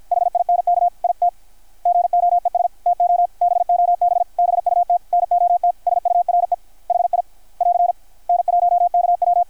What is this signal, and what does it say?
Morse code, heard as a keyed tone, about 53 wpm, 720 Hz. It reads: HINK IT GOES TO BOB BUT NOT SURE HI <AR> N1KW
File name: archived / CRSnet